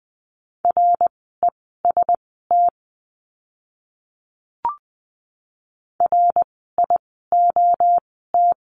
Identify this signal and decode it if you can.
Morse code, heard as a keyed tone, about 20 wpm, spaced as for 14 wpm, 700 Hz. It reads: REST RIOT
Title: 061 Lesson 11f - Letter h - Words 20wpm